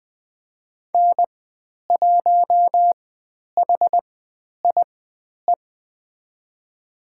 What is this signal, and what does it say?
Morse code, heard as a keyed tone, about 20 wpm, spaced as for 10 wpm, 700 Hz. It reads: N1HIE
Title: 109 Lesson 18g - Letter m - Call Signs 20wpm